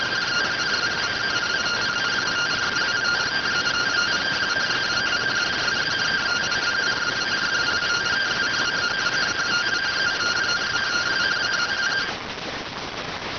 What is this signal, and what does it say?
Two-tone teleprinter signal, about 170 Hz shift.